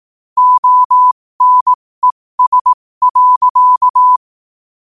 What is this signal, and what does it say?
Morse code, heard as a keyed tone, about 18 wpm, spaced as for 15 wpm, 1000 Hz. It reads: ONES.
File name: qotdfiles / 15wpm